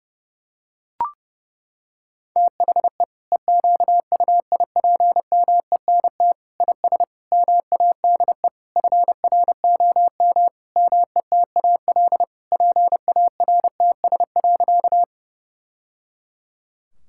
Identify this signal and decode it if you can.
Morse code, heard as a keyed tone, about 30 wpm, 700 Hz. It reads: THE EQUIPMENT IS MADE FROM METAL PARTS.
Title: Sentences from Top 1000 Words - Review Part 6 - 30wpm